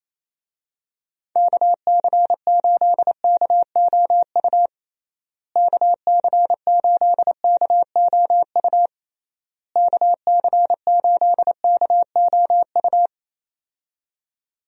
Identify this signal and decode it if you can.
Morse code, heard as a keyed tone, about 28 wpm, 700 Hz. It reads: KC8KOU KC8KOU KC8KOU
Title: USA Call Signs - 6 Characters - Repeated 3x - 28wpm